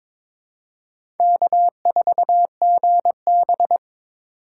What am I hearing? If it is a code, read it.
Morse code, heard as a keyed tone, about 22 wpm, 700 Hz. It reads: K4GB
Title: USA Call Signs - 4 Characters - 3x Character Spacing - 22wpm